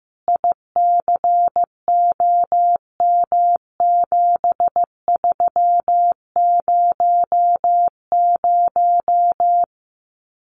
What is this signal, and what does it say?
Morse code, heard as a keyed tone, about 15 wpm, 700 Hz. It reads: ICOM7300
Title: Common QSO Phrases Encore 15wpm